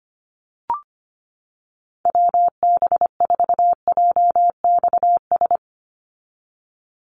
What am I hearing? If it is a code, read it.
Morse code, heard as a keyed tone, about 25 wpm, 700 Hz. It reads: WB4JXS
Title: USA Call Signs 25wpm